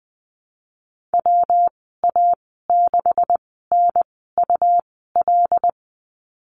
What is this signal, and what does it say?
Morse code, heard as a keyed tone, about 20 wpm, spaced as for 14 wpm, 700 Hz. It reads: WA6NUL
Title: 207 Lesson 33g - Letter j - Call Signs 20wpm